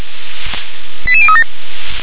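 Five-tone selective call (ZVEI), heard as quick runs of five tones, tones of about 70 ms. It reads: 88037
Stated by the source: Five-tone ZVEI selective call received by SDR.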